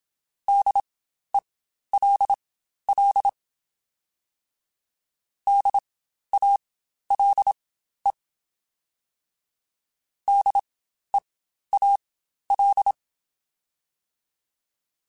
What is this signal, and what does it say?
Morse code, heard as a keyed tone, about 27 wpm, spaced as for 12 wpm, 800 Hz. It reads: DELL DALE DEAL
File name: Spaced Alphas / Spaced Alphas mp3s / Ds sp @25